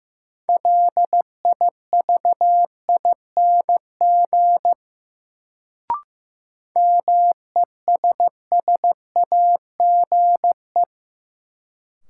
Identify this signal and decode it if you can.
Morse code, heard as a keyed tone, about 15 wpm, 700 Hz. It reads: LIVING MESSAGE
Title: Sentences from Top 1100 Words - New Word Drill Alt - 15wpm